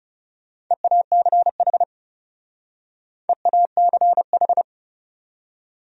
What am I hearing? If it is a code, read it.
Morse code, heard as a keyed tone, about 37 wpm, 700 Hz. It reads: EACH EACH
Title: Speed-Racing Top 100 Words 30wpm